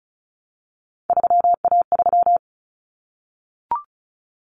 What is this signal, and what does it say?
Morse code, heard as a keyed tone, about 34 wpm, 700 Hz. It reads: 3A3
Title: Common Cut Numbers - 3-Digit 35wpm